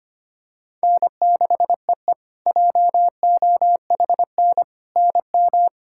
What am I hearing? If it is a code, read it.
Morse code, heard as a keyed tone, about 25 wpm, 700 Hz. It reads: N6EE JOHN NM
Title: K1USN SST QSO Practice - 25wpm